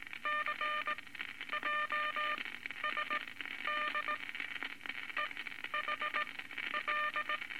Morse code, heard as a keyed tone, about 20 wpm, spaced as for 12 wpm, 1200 Hz, with noise. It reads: C JSD EH L
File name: Morse Code Records / 19 A LCM 166 Ten Words Per Minute